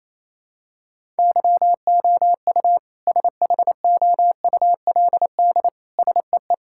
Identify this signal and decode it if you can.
Morse code, heard as a keyed tone, about 28 wpm, 700 Hz. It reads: YOU SHOULD SEE
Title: Sentences from Top 1600 Words 28wpm